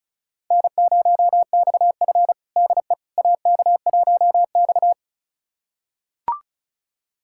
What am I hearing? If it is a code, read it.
Morse code, heard as a keyed tone, about 34 wpm, 700 Hz. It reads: N0XF DE AK1X
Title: USA Callsign DE Callsign - 4 Characters - 35wpm